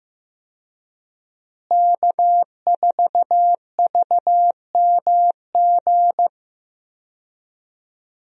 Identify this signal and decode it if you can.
Morse code, heard as a keyed tone, about 15 wpm, 700 Hz. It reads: K4VMG
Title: USA Call Signs - 5 Characters - Repeated 2x - 15wpm